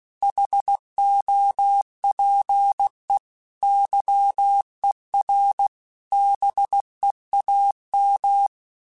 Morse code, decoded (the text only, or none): HOPE YER BEAM